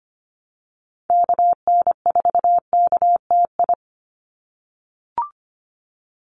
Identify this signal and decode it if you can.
Morse code, heard as a keyed tone, about 25 wpm, 700 Hz. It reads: KN4KTI
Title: USA Call Signs 25wpm